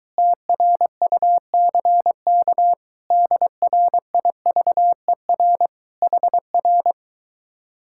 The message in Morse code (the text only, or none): TRUCK DRIVER HR